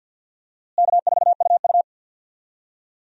Morse code, heard as a keyed tone, about 52 wpm, 700 Hz. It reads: K4UU
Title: USA Call Signs - 4 Characters - Repeated 2x - 50wpm